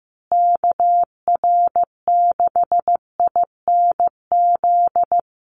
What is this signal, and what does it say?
Morse code, heard as a keyed tone, about 15 wpm, 700 Hz. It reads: KR6INZ
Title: USA Call Signs Repeated 2x 15wpm